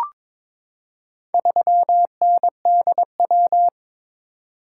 Morse code, heard as a keyed tone, about 22 wpm, 700 Hz. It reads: 3NDW
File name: practice / 22wpm / Four Letter-Number Combinations - 22wpm